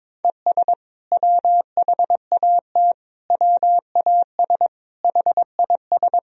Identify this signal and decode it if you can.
Morse code, heard as a keyed tone, about 22 wpm, 700 Hz. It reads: ES WHAT WAS HIS
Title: Sentences from Top 200 Words and Abbreviations 22wpm